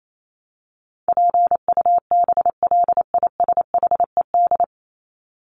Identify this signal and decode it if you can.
Morse code, heard as a keyed tone, about 28 wpm, 700 Hz. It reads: PUBLISHED
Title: ED Suffix - Top 100 Words with Base Words 28wpm